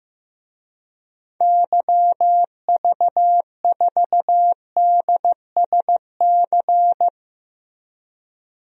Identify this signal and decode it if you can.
Morse code, heard as a keyed tone, about 16 wpm, 700 Hz. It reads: YV4DSC
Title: DXCC-100 Call Signs - 6 Characters - Repeated 2x - 15wpm